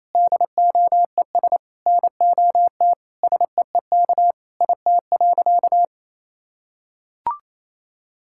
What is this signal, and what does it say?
Morse code, heard as a keyed tone, about 28 wpm, 700 Hz. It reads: DOES NOT SEEK IT.
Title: Sentences from Top 1600 Words 28wpm